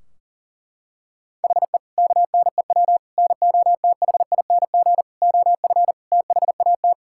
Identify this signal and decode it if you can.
Morse code, heard as a keyed tone, about 37 wpm, 700 Hz. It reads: HE KNEW NOTHING OF THAT
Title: Sentences from Top 1600 Words 40wpm